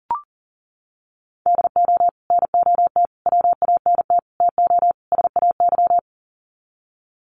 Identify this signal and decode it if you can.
Morse code, heard as a keyed tone, about 40 wpm, 700 Hz. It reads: DO NOT WANT TO SAY
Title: Sets of 5 Words Encore 40wpm